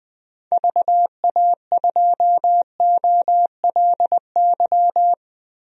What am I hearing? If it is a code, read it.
Morse code, heard as a keyed tone, about 21 wpm, 700 Hz. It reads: VA2OLY
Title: DXCC-100 Call Signs - 6 Characters - Repeated 2x - 20wpm